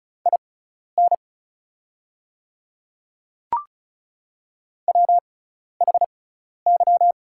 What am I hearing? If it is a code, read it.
Morse code, heard as a keyed tone, about 34 wpm, spaced as for 12 wpm, 690 Hz. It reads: IN WHY
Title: 6x Character Spacing - Top 200 Words - 35wpm